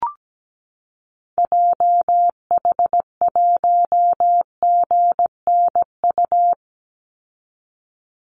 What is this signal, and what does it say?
Morse code, heard as a keyed tone, about 17 wpm, 700 Hz. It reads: JH1GNU
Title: Call Signs - 17wpm